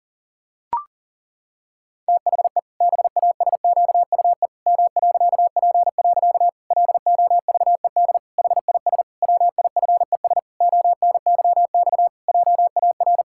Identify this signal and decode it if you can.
Morse code, heard as a keyed tone, about 40 wpm, 700 Hz. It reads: THE BASQUE M.P. LOVED HIS WIFES ONYX JAR
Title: Pangram Challenge v2 - 40wpm